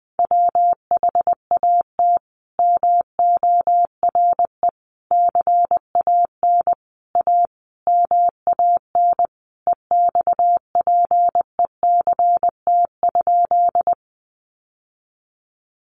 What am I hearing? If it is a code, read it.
Morse code, heard as a keyed tone, about 20 wpm, 700 Hz. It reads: WHAT MORE CAN A MAN EXPECT?
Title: Sentences from Top 1100 Words 20wpm